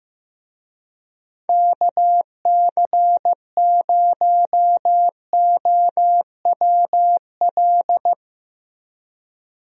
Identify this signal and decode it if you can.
Morse code, heard as a keyed tone, about 15 wpm, 700 Hz. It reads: KC0OWL
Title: USA Call Signs - 6 Characters - Repeated 3x - 15wpm